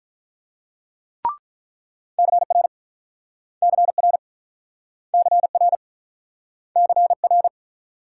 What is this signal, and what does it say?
Morse code, heard as a keyed tone, about 47 wpm, 700 Hz. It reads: CR CR CR CR